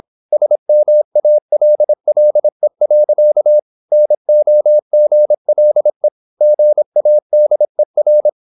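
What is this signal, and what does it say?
Morse code, heard as a keyed tone, about 26 wpm, 600 Hz. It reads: SMALLE. NOGLE GADER
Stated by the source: High-speed (QRQ) Morse at 26 wpm.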